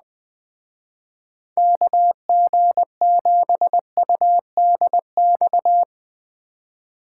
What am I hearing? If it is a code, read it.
Morse code, heard as a keyed tone, about 20 wpm, 700 Hz. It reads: KG7UDX